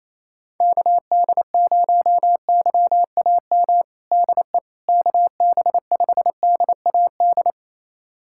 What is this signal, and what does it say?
Morse code, heard as a keyed tone, about 28 wpm, 700 Hz. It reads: KD0YAM DE KB5DAD
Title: USA Callsign DE Callsign - 6 Characters - 28wpm